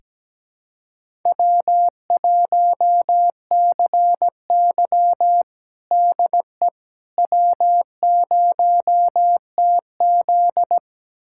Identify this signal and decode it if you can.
Morse code, heard as a keyed tone, about 17 wpm, 700 Hz. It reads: W1CY DE W0TZ
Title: USA Callsign DE Callsign - 4 Characters - 17wpm